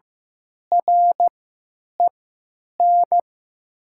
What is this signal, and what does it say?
Morse code, heard as a keyed tone, about 15 wpm, spaced as for 9 wpm, 700 Hz. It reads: REN
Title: 3x Character Spacing - Top 300 Words - 15wpm